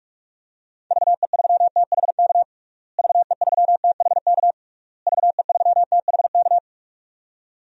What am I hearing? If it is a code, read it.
Morse code, heard as a keyed tone, about 41 wpm, 700 Hz. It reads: VE3THK VE3THK VE3THK
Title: DXCC-100 Call Signs - 6 Characters - Repeated 3x - 45wpm